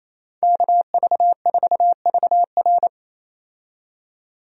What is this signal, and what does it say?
Morse code, heard as a keyed tone, about 28 wpm, 700 Hz. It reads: KV4VR
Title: USA Call Signs - 5 Characters - 28wpm